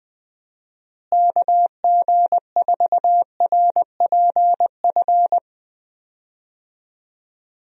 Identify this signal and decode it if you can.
Morse code, heard as a keyed tone, about 20 wpm, 700 Hz. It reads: KG4RPF
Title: USA Call Signs - 6 Characters - Repeated 3x - 20wpm